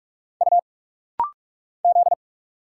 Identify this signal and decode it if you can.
Morse code, heard as a keyed tone, about 46 wpm, 690 Hz. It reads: U Z
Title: Single Letters - Rapid-Fire 45wpm